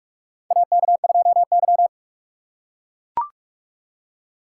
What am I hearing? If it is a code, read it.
Morse code, heard as a keyed tone, about 42 wpm, 700 Hz. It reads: AK2Y
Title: USA Call Signs - 4 Characters - 6x Character Spacing - 45wpm